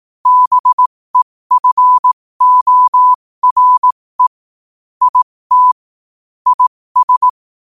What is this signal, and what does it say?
Morse code, heard as a keyed tone, about 18 wpm, spaced as for 15 wpm, 1000 Hz. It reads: BEFORE IT IS
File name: qotdfiles / 15wpm